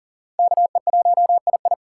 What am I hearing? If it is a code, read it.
Morse code, heard as a keyed tone, about 40 wpm, 700 Hz. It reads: KE1II
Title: USA Call Signs - 5 Characters - 3x Character Spacing - 40wpm